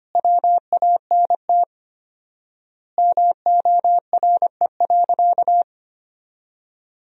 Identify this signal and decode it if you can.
Morse code, heard as a keyed tone, about 25 wpm, 700 Hz. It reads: WANT MORE.
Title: 4x Word-Spacing - Sentences from Top 200 Words - 25wpm